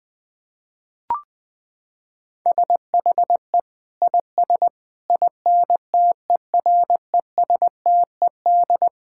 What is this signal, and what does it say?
Morse code, heard as a keyed tone, about 20 wpm, 700 Hz. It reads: SHE IS INTERESTED
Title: Sentences from Top 1000 Words - Review Part 6 - 20wpm